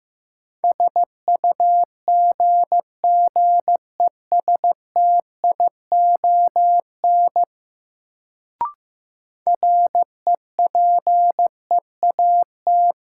Morse code, heard as a keyed tone, about 15 wpm, 700 Hz. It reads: SUGGESTION REPEAT
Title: Sentences from Top 1600 Words - New Word Drill Alt - 15wpm